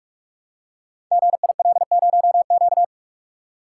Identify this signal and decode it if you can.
Morse code, heard as a keyed tone, about 44 wpm, 700 Hz. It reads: GIL0Q
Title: Code Groups - 5 Letter:Numbers - 45wpm